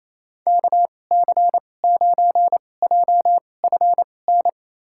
Morse code, heard as a keyed tone, about 28 wpm, spaced as for 20 wpm, 700 Hz. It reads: KC9JFN
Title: USA Call Signs - 6 Characters - 2x Character Spacing - 28wpm